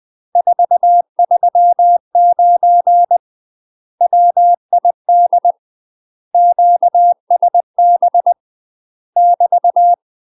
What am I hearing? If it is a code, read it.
Morse code, heard as a keyed tone, about 20 wpm, 700 Hz. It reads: 439 WID QSB <BT>
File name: UK MASTER QSOs TRAINING / 20WPM / UK MASTER QSOs 0073